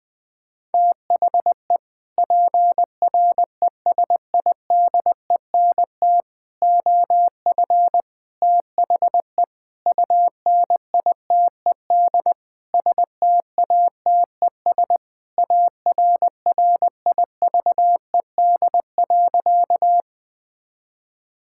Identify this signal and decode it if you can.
Morse code, heard as a keyed tone, about 20 wpm, 700 Hz. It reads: THE PRESIDENT OF THE UNITED STATES ARRIVED.
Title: Sentences with 6-Word Phrases - 20wpm